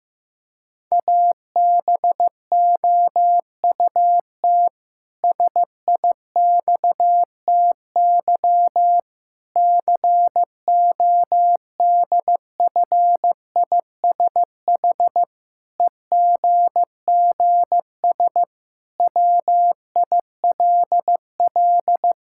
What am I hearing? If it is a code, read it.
Morse code, heard as a keyed tone, about 15 wpm, 700 Hz. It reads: ABOUT SIXTY CODFISH EGGS WILL
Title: Pangram Challenge v2 - 15wpm